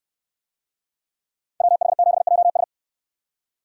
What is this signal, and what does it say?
Morse code, heard as a keyed tone, about 68 wpm, 700 Hz. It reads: WH7JS